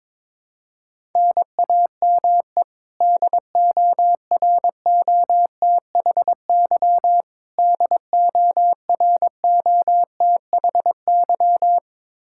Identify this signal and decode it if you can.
Morse code, heard as a keyed tone, about 23 wpm, 700 Hz. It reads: NAME DOROTHY DOROTHY
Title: Practice QSO - Exchange 1 - Beginning - 22wpm